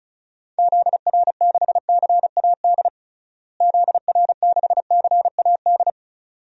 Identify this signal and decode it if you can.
Morse code, heard as a keyed tone, about 36 wpm, 700 Hz. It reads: ZR6CAD ZR6CAD
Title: DXCC-100 Call Signs - 6 Characters - Repeated 2x - 35wpm